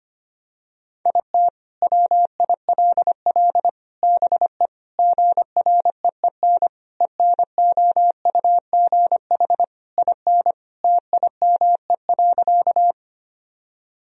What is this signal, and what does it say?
Morse code, heard as a keyed tone, about 25 wpm, 700 Hz. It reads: IT WILL BE GREEN ENOUGH IN TIME.
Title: Sentences from Top 1000 Words - Review Part 2 - 25wpm